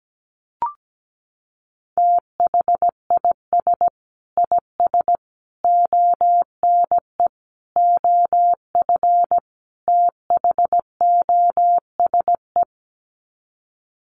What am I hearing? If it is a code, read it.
Morse code, heard as a keyed tone, about 17 wpm, 700 Hz. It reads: THIS IS ONE OF THOSE
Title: Sets of 5 Words Encore2 17wpm